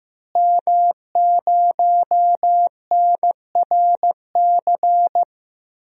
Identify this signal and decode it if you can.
Morse code, heard as a keyed tone, about 15 wpm, 700 Hz. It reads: M0NRC
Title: DXCC-100 Call Signs - 5 Characters - Repeated 3x - 15wpm